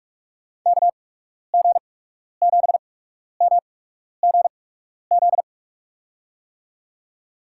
Morse code, heard as a keyed tone, about 46 wpm, spaced as for 12 wpm, 700 Hz. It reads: KG7MGZ
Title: USA Call Signs - 6 Characters - 8x Character Spacing - 45wpm